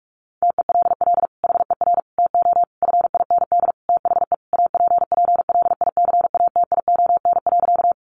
Morse code, heard as a keyed tone, about 44 wpm, 700 Hz. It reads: TELL HER TO FIND THE APPLICATION.
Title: Sentences from Top 1200 Words 45wpm